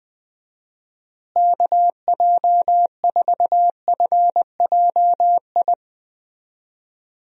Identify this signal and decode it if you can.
Morse code, heard as a keyed tone, about 20 wpm, 700 Hz. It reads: KJ4FJI
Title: USA Call Signs - 6 Characters - Repeated 2x - 20wpm